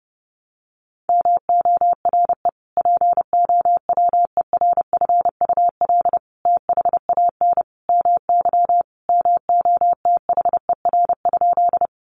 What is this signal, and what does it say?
Morse code, heard as a keyed tone, about 30 wpm, 700 Hz. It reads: MORE POWERFUL THAN MY MOTHER?
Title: Sentences from Top 1300 Words 30wpm